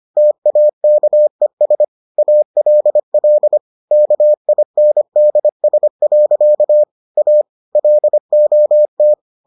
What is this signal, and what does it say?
Morse code, heard as a keyed tone, about 25 wpm, 600 Hz. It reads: TAKES ALL KINDS. A LOT